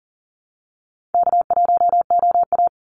Morse code, heard as a keyed tone, about 40 wpm, 700 Hz. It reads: K1OA
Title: Easy Call Signs - 40wpm